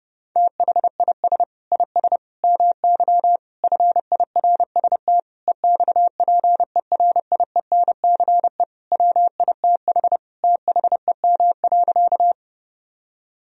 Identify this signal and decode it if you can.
Morse code, heard as a keyed tone, about 32 wpm, 700 Hz. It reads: THIS IS MY FIRST EXPERIENCE WITH THEM.
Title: Sentences from Top 1000 Words - Review Part 3 - 30wpm